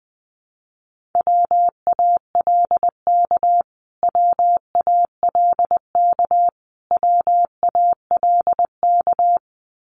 Morse code, heard as a keyed tone, about 20 wpm, 700 Hz. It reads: WALK WALK WALK
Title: Top 300 Words Repeated 20wpm